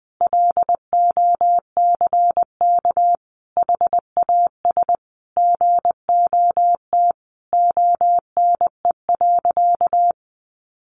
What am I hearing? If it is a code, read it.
Morse code, heard as a keyed tone, about 20 wpm, 700 Hz. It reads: LOCK HAS GOT ONE.
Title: Sentences from Top 1300 Words 20wpm